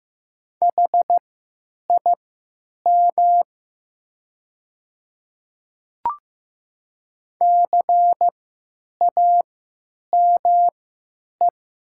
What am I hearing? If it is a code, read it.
Morse code, heard as a keyed tone, about 15 wpm, spaced as for 9 wpm, 700 Hz. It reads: HIM CAME
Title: 3x Character Spacing - Top 200 Words - 15wpm